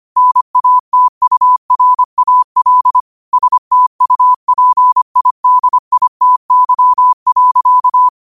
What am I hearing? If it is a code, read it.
Morse code, heard as a keyed tone, about 25 wpm, 1000 Hz. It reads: NATURAL STUPIDITY.